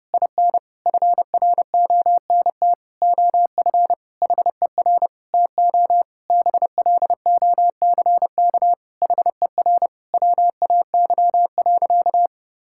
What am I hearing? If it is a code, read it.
Morse code, heard as a keyed tone, about 30 wpm, 700 Hz. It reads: IN FRONT OF HER TO BLOCK HER WAY.
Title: Sentences from Top 1600 Words 30wpm